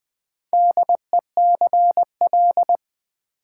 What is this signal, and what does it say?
Morse code, heard as a keyed tone, about 20 wpm, 700 Hz. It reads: DECL
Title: Build A Word - 4 Letters At A Time - 11-Letter Words - 20wpm